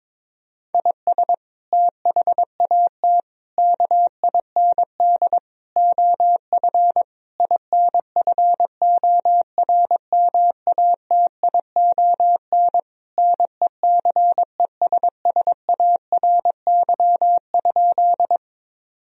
Morse code, heard as a keyed tone, about 22 wpm, 700 Hz. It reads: IS THAT KIND OF INFORMATION NECESSARY?
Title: Sentences from Top 1000 Words - Review Part 3 - 22wpm